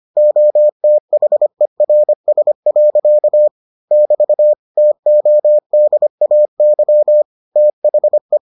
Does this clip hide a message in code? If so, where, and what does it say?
Yes: Morse code from 0.2 to 8.4 s, heard as a keyed tone, about 25 wpm, 600 Hz. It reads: OTHERS. <BT> TODAY THE